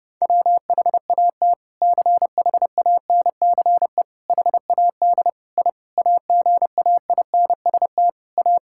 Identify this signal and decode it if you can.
Morse code, heard as a keyed tone, about 30 wpm, 700 Hz. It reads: WHAT CHANCE HAD I AGAINST A
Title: Sentences from Top 1000 Words - Review Part 6 - 30wpm